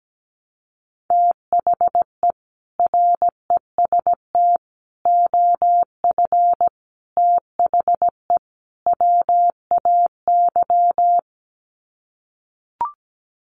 Morse code, heard as a keyed tone, about 17 wpm, 690 Hz. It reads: THE REST OF THE WAY
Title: Sets of 5 Words Encore 17wpm